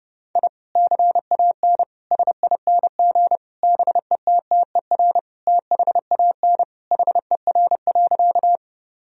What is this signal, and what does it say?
Morse code, heard as a keyed tone, about 30 wpm, 700 Hz. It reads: I CAN SING BETTER THAN HER.
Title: Sentences from Top 1000 Words - Review Part 2 - 30wpm